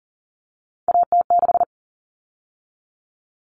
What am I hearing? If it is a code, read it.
Morse code, heard as a keyed tone, about 40 wpm, 700 Hz. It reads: AT6
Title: Common Cut Numbers - 3-Digit 40wpm